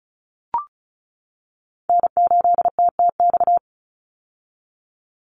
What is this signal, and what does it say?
Morse code, heard as a keyed tone, about 34 wpm, 700 Hz. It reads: N8TTX
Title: USA Call Signs 35wpm